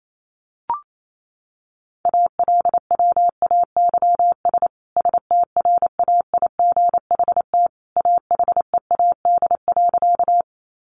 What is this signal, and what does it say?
Morse code, heard as a keyed tone, about 28 wpm, 700 Hz. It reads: ALWAYS STRAIGHT AHEAD.
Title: Sentences from Top 1100 Words 28wpm